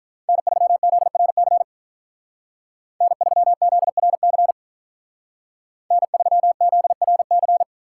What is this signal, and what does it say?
Morse code, heard as a keyed tone, about 46 wpm, 700 Hz. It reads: N3ZRC N3ZRC N3ZRC